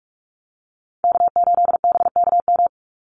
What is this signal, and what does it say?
Morse code, heard as a keyed tone, about 44 wpm, 700 Hz. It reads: K8BKM
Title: Call Signs Repeated 2x - 45wpm